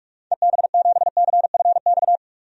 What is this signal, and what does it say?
Morse code, heard as a keyed tone, about 46 wpm, 700 Hz. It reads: EB7CFX
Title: DXCC-100 Call Signs - 6 Characters - 45wpm